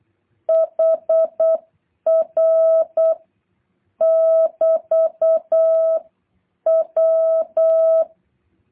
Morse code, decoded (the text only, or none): HR <BT> W